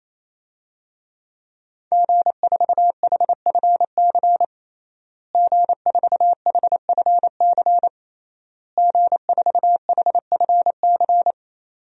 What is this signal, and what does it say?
Morse code, heard as a keyed tone, about 28 wpm, 700 Hz. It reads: G4HFC G4HFC G4HFC
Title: DXCC-100 Call Signs - 5 Characters - Repeated 3x - 28wpm